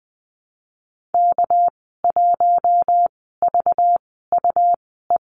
Morse code, heard as a keyed tone, about 20 wpm, spaced as for 15 wpm, 700 Hz. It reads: K1VUE
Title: 207 Lesson 33g - Letter j - Call Signs 20wpm